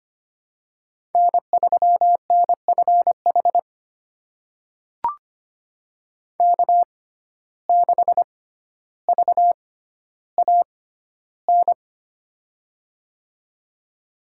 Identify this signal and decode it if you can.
Morse code, heard as a keyed tone, about 25 wpm, spaced as for 13 wpm, 700 Hz. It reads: N3NFH K6VAN
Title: USA Call Signs - 5 Characters - 6x Character Spacing - 25wpm